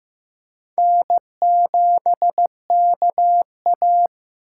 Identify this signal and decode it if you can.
Morse code, heard as a keyed tone, about 18 wpm, 700 Hz. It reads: N7KA
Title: Speed Racing - USA 1x2 Call Signs 15wpm